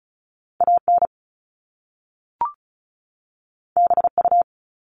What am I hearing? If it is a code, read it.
Morse code, heard as a keyed tone, about 36 wpm, 700 Hz. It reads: AN BU
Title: Top Two Letter Combinations in Words 35wpm